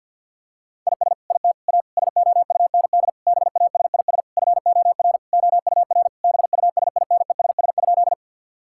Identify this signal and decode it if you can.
Morse code, heard as a keyed tone, about 50 wpm, 700 Hz. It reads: IS IT A SOUND BASIS FOR OUR BUSINESS?
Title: Sentences from Top 1000 Words - Review Part 5 - 50wpm